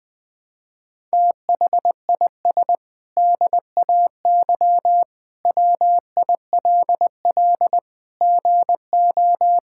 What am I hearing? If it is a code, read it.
Morse code, heard as a keyed tone, about 21 wpm, 700 Hz. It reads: THIS DAY WILL GO
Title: Sentences with 8-Word Phrases - Spoken Beginning - 20wpm